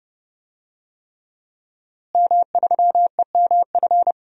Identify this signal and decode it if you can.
Morse code, heard as a keyed tone, about 30 wpm, 700 Hz. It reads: M3EMF
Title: DXCC-100 Call Signs - 5 Characters - Repeated 3x - 30wpm